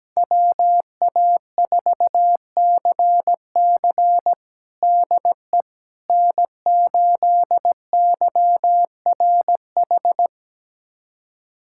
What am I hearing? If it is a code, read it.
Morse code, heard as a keyed tone, about 17 wpm, 700 Hz. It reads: WA4CC DE N8YRH
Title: USA Callsign DE Callsign - 5 Characters - 17wpm